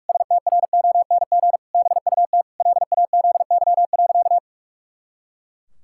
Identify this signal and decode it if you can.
Morse code, heard as a keyed tone, about 46 wpm, 700 Hz. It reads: STRONG BUT LAZY.
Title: Pangram Challenge v2 - 45wpm